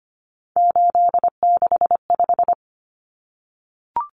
Morse code, heard as a keyed tone, about 25 wpm, 700 Hz. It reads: 865
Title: Common Cut Numbers - 3-Digit 25wpm